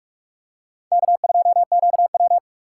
Morse code, heard as a keyed tone, about 44 wpm, 700 Hz. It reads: K2QW